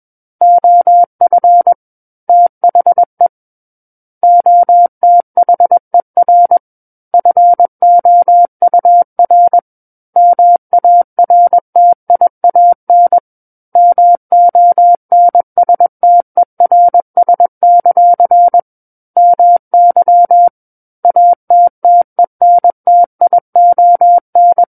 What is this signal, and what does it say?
Morse code, heard as a keyed tone, about 21 wpm, 700 Hz. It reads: OF THE OTHER FOUR MARTIAN MONSTERS; MY ATTENTION
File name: Book One / 12-Chapter 12 - 21 WPM